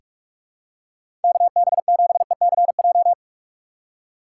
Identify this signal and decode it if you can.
Morse code, heard as a keyed tone, about 45 wpm, 700 Hz. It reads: KB7ECJ
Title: USA Call Signs - 6 Characters - 3x Character Spacing - 45wpm